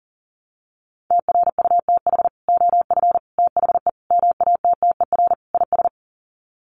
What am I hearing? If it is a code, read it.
Morse code, heard as a keyed tone, about 40 wpm, 700 Hz. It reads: TRUTH OF THE MATTER IS
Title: Sets of 5 Words Encore2 40wpm